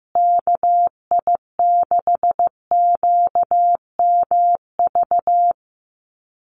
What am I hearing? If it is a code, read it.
Morse code, heard as a keyed tone, about 15 wpm, 700 Hz. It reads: KI6QMV
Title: USA Call Signs Repeated 3x 15wpm